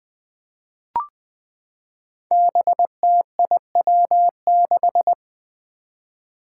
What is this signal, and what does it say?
Morse code, heard as a keyed tone, about 20 wpm, 700 Hz. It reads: BTIW6
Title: Five Challenging Letter-Number Combinations 20wpm